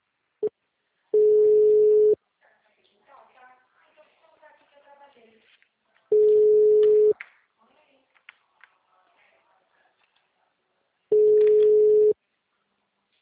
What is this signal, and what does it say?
Ringback tone (Europe, 425 Hz), with speech.
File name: out- -2026- - - .14